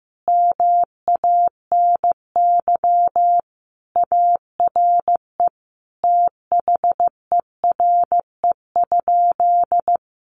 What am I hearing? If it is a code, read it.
Morse code, heard as a keyed tone, about 15 wpm, 700 Hz. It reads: MANY ARE THERE?
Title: Sentences from Top 100 Words 15wpm